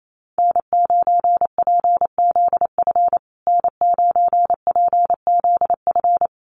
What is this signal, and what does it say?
Morse code, heard as a keyed tone, about 28 wpm, 700 Hz. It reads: N9PZF N9PZF
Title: USA Call Signs Repeated 3x 28wpm